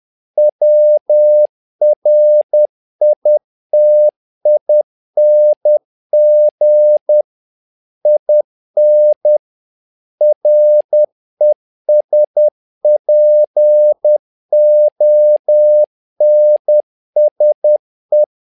Morse code, heard as a keyed tone, about 10 wpm, 600 Hz. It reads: WRITING IN RESPONSE
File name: learning center / tower trouble / tower trouble 10wpm